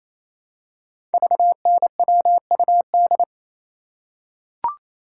Morse code, heard as a keyed tone, about 29 wpm, 700 Hz. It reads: VNWUD
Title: Code Groups - 5 Letters - 28wpm